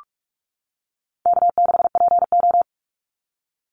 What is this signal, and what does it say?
Morse code, heard as a keyed tone, about 44 wpm, 700 Hz. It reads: K6PO
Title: Easy Call Signs - 45wpm